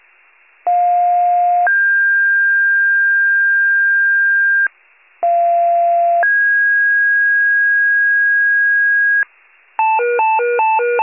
Two-tone page, heard as a one-second tone, then a three-second tone, about 710 then 1600 Hz, about 690 then 1650 Hz. It is heard from 0.7 to 9.2 s.
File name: wcalert